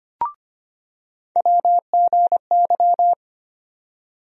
Morse code, heard as a keyed tone, about 25 wpm, 700 Hz. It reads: WGY
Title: USA Call Sign Postfixes - 3 Letters 25wpm